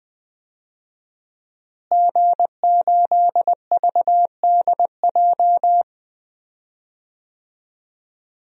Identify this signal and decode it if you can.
Morse code, heard as a keyed tone, about 20 wpm, 700 Hz. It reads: G8VDJ
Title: DXCC-100 Call Signs - 5 Characters - Repeated 2x - 20wpm